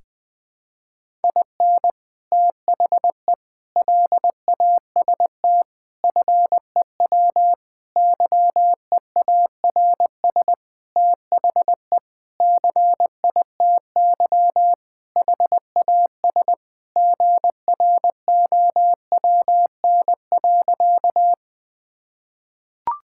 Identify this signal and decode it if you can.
Morse code, heard as a keyed tone, about 20 wpm, 700 Hz. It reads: IN THE LAST FEW YEARS THE CITY HAS GROWN.
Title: Sentences with 5-Word Phrases - 20wpm